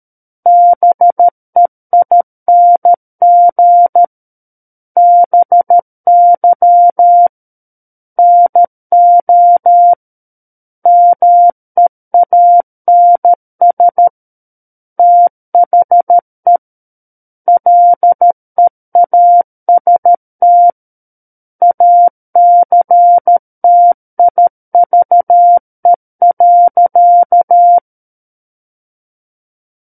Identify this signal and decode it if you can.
Morse code, heard as a keyed tone, about 13 wpm, 700 Hz. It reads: BEING BY NO MEANS THE LEAST ACTIVE.
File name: Book One / 04-Chapter 4 - 13 WPM